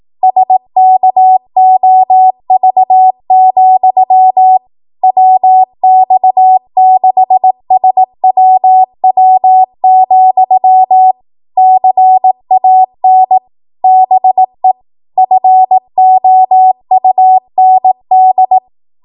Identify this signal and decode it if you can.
Morse code, heard as a keyed tone, about 18 wpm, 750 Hz. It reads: SKOV, WX6SWW, CAN BE FOUND
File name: Morse / 18WPM